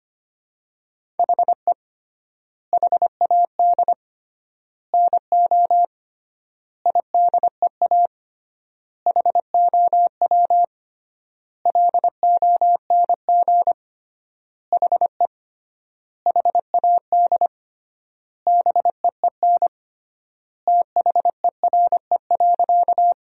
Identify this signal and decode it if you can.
Morse code, heard as a keyed tone, about 25 wpm, 700 Hz. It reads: HE HAD NO IDEA HOW LONG HE HAD BEEN THERE.
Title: 3x Word-Spacing - Sentences from Top 300 Words - 25wpm